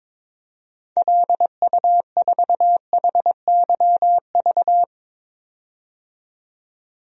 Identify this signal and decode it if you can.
Morse code, heard as a keyed tone, about 22 wpm, 700 Hz. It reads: LU4HYV
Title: DXCC-100 Call Signs - 6 Characters - 22wpm